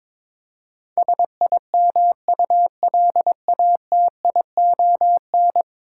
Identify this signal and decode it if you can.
Morse code, heard as a keyed tone, about 22 wpm, 700 Hz. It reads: SIMULATION
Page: morse-code-ninja-practice-22wpm